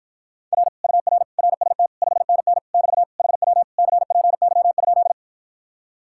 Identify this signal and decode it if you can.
Morse code, heard as a keyed tone, about 54 wpm, 700 Hz. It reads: R UR RST 5NN <BT> HW CPY?